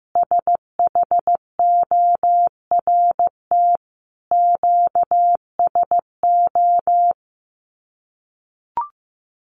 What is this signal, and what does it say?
Morse code, heard as a keyed tone, about 15 wpm, 700 Hz. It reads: SHORT QSO
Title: Common QSO Phrases 15wpm